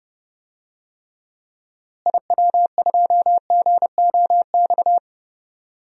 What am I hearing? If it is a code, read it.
Morse code, heard as a keyed tone, about 30 wpm, 700 Hz. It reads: IW2GOX
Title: DXCC-100 Call Signs - 6 Characters - Repeated 2x - 30wpm